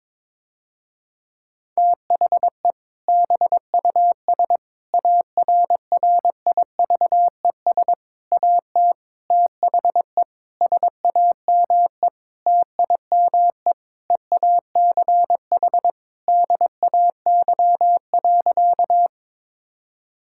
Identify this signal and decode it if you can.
Morse code, heard as a keyed tone, about 22 wpm, 700 Hz. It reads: THE BUS ARRIVES AT THE SAME TIME EACH DAY.
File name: Difficult Sentences with Hints - 22wpm